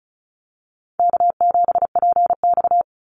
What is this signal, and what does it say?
Morse code, heard as a keyed tone, about 34 wpm, 700 Hz. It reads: K7PX
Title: NA SOTA Activator Callsigns 35wpm